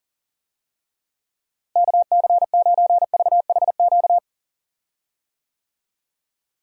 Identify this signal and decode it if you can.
Morse code, heard as a keyed tone, about 38 wpm, 700 Hz. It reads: KC9VHQ
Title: USA Call Signs - 6 Characters - Repeated 2x - 40wpm